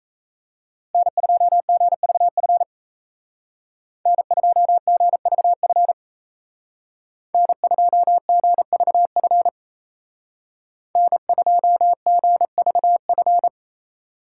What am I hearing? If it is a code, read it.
Morse code, heard as a keyed tone, about 36 wpm, 700 Hz. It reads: N2GVF N2GVF N2GVF N2GVF